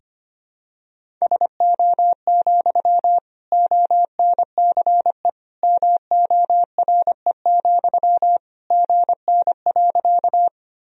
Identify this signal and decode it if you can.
Morse code, heard as a keyed tone, about 25 wpm, 700 Hz. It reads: SO, ONCE MORE, GN.
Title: Sentences from Top 300 Words and Abbreviations 25wpm